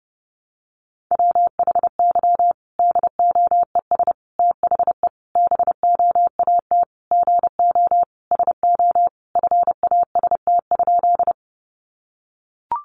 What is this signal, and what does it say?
Morse code, heard as a keyed tone, about 28 wpm, 700 Hz. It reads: WHY DOES THE BOAT GO SO FAST?
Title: Sentences from Top 500 Words 30wpm